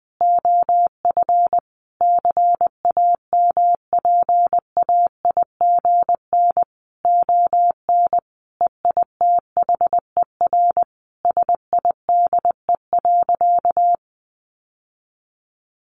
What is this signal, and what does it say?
Morse code, heard as a keyed tone, about 20 wpm, 700 Hz. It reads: OF CAMPAIGN ON EITHER SIDE.
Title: Sentences from Top 1200 Words 20wpm